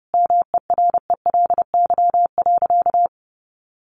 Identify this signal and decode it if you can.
Morse code, heard as a keyed tone, about 30 wpm, 700 Hz. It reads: MERELY.
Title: Sentences from Top 1400 Words 30wpm